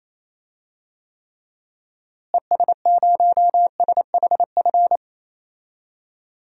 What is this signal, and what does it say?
Morse code, heard as a keyed tone, about 28 wpm, 700 Hz. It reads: ES0SHF
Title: DXCC-100 Call Signs - 6 Characters - Repeated 3x - 28wpm